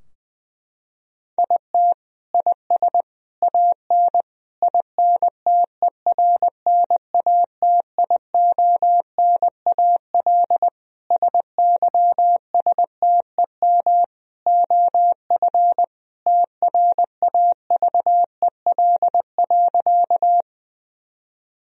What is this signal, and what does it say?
Morse code, heard as a keyed tone, about 20 wpm, 700 Hz. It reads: IT IS AN INTERNATIONAL SYSTEM OF TRAVEL.
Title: Sentences from Top 1000 Words - Review Part 4 - 20wpm